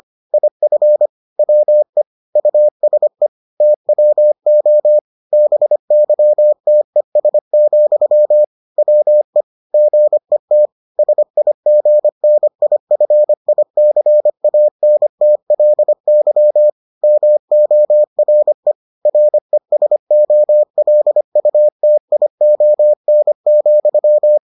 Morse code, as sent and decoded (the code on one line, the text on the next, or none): .. ..-. / .-- . / ..- ... / - .-- --- / -... -.-- - . ... --..-- / .-- . / --. - / ... .. --. -. .. ..-. .. -.-. .- -. - .-.. -.-- / -- --- .-. / .-. ... --- .-.. ..- - .. --- -. --..--
IF WE USE TWO BYTES, WE GET SIGNIFICANTLY MORE RESOLUTION,